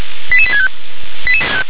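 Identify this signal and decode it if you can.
Five-tone selective call (ZVEI), heard as quick runs of five tones, tones of about 70 ms. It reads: 88865
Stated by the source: ZVEI five-tone selective call received on an SDR.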